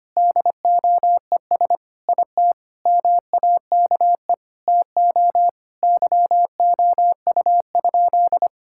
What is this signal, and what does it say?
Morse code, heard as a keyed tone, about 25 wpm, 700 Hz. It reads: DOES IT MAKE TO YOU?
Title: Sentences from Top 1000 Words - Review Part 5 - 25wpm